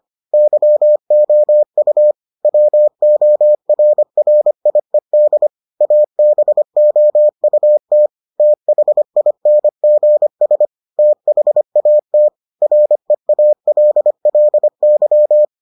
Morse code, heard as a keyed tone, about 25 wpm, 600 Hz. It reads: YOU WORRIED ABOUT THINGS THAT REALLY